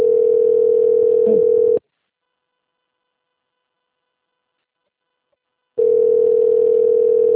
Ringback tone (North America), 440 + 480 Hz.